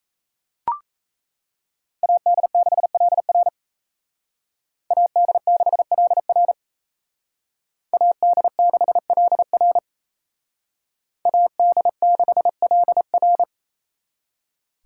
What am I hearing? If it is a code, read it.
Morse code, heard as a keyed tone, about 33 wpm, 700 Hz. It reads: AD6LR AD6LR AD6LR AD6LR
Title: Speed Racing - USA 2x2 Call Signs 28wpm